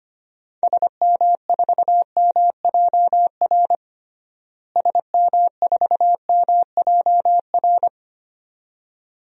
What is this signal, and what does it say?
Morse code, heard as a keyed tone, about 25 wpm, 700 Hz. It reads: SM4MJR SM4MJR
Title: DXCC-100 Call Signs - 6 Characters - Repeated 2x - 25wpm